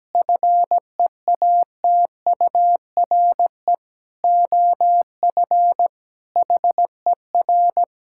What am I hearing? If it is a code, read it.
Morse code, heard as a keyed tone, about 17 wpm, 700 Hz. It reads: FEATURE OF HER
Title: Sentences from Top 1000 Words - Review Part 5 - 17wpm